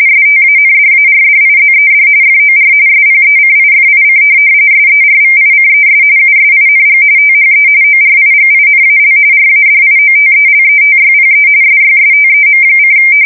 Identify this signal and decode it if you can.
RTTY, about 45.45 baud. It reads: STANDARDIZED FOR SPECIFIC SERVICES USING THE ORIGINAL RADIOTELETYPE SYSTEM:
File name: fsk170 / teletype-RTTY